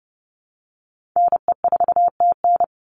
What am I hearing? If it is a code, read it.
Morse code, heard as a keyed tone, about 30 wpm, 700 Hz. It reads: NE4TN
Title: NA SOTA Chaser Callsigns 30wpm